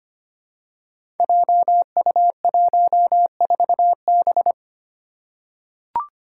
Morse code, heard as a keyed tone, about 25 wpm, 700 Hz. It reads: JU14B
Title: Five Challenging Letter-Number Combinations 25wpm